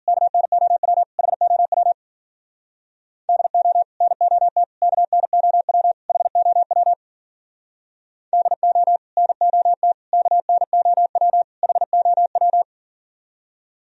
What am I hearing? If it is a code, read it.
Morse code, heard as a keyed tone, about 51 wpm, 700 Hz. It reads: KNOW HOW DO NOT KNOW HOW DO NOT KNOW HOW